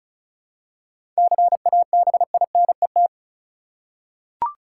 Morse code, heard as a keyed tone, about 36 wpm, 700 Hz. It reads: CABINET
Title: Seven Letter Words - 35wpm